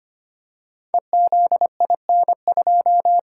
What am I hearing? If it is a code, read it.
Morse code, heard as a keyed tone, about 25 wpm, 700 Hz. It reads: EZIN2
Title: Code Groups - 5 Letter:Numbers - 25wpm